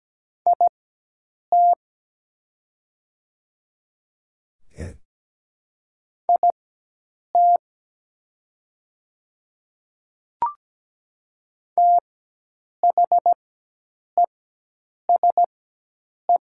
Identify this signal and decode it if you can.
Morse code, heard as a keyed tone, about 17 wpm, spaced as for 8 wpm, 690 Hz. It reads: IT IT THESE
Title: 4x Character Spacing - Top 300 Words - 17wpm